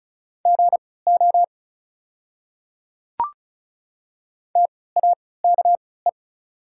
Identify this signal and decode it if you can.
Morse code, heard as a keyed tone, about 36 wpm, spaced as for 20 wpm, 700 Hz. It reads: GO TAKE
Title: 3x Character Spacing - Top 300 Words - 35wpm